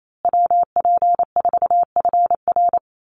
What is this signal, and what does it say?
Morse code, heard as a keyed tone, about 28 wpm, 700 Hz. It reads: WP4FR
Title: USA Call Signs Repeated 2x 28wpm